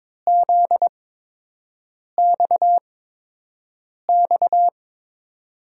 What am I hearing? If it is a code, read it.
Morse code, heard as a keyed tone, about 22 wpm, spaced as for 6 wpm, 700 Hz. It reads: ZXX
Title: USA Call Signs - 8x Character Spacing - 22wpm